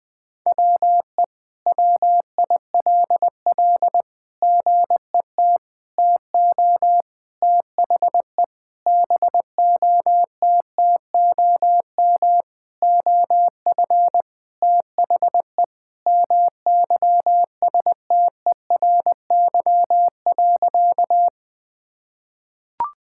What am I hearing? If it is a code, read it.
Morse code, heard as a keyed tone, about 20 wpm, 700 Hz. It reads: WE WILL GET TO THE BOTTOM OF THE MYSTERY.
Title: Sentences with 6-Word Phrases - 20wpm